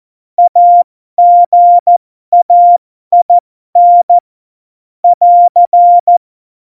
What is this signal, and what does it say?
Morse code, heard as a keyed tone, about 14 wpm, spaced as for 12 wpm, 700 Hz. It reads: AGAIN <AR>
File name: Morsekurs / Witze